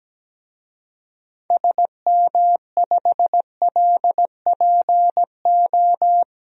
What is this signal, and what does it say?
Morse code, heard as a keyed tone, about 17 wpm, 700 Hz. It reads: SM5LPO
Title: DXCC-100 Call Signs - 6 Characters - 17wpm